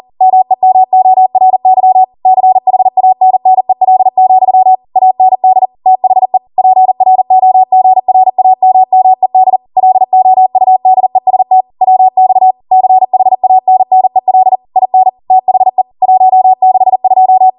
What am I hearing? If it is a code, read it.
Morse code, heard as a keyed tone, about 40 wpm, 750 Hz. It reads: MEMORY CHANNEL, AND THE PROGRAMMED LOUDEST WX CHANNEL IN THE 162
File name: Morse / 40WPM